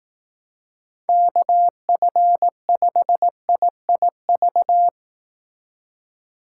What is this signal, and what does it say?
Morse code, heard as a keyed tone, about 18 wpm, 700 Hz. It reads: KF5IIV
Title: Speed Racing - USA 2x3 Call Signs 15wpm